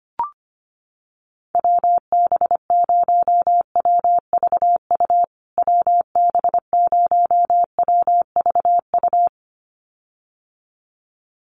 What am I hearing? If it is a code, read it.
Morse code, heard as a keyed tone, about 25 wpm, 700 Hz. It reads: WB0WVU WB0WVU
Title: USA Call Signs Repeated 2x 25wpm